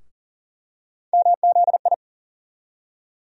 Morse code, heard as a keyed tone, about 40 wpm, 700 Hz. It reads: MZI
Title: USA Call Sign Postfixes - 3 Letters 40wpm